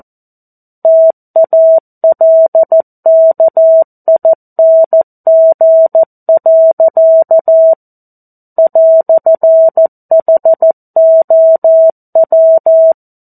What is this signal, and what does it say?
Morse code, heard as a keyed tone, about 14 wpm, 650 Hz. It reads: TALKING. "HOW
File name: 02-Chapter-02 14 WPM